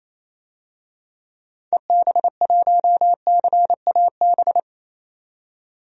Morse code, heard as a keyed tone, about 28 wpm, 700 Hz. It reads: EB1CAB
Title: DXCC-100 Call Signs - 6 Characters - 28wpm